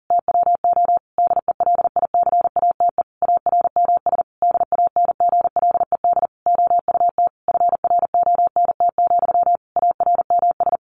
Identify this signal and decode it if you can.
Morse code, heard as a keyed tone, about 40 wpm, 700 Hz. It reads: TWO DELICATE ARMS DANGLED OUT FRONT, ARMS
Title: A Sound Of Thunder 40wpm